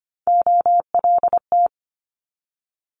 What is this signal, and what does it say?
Morse code, heard as a keyed tone, about 25 wpm, 700 Hz. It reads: OLT
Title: Multi-letter Phonemes 25wpm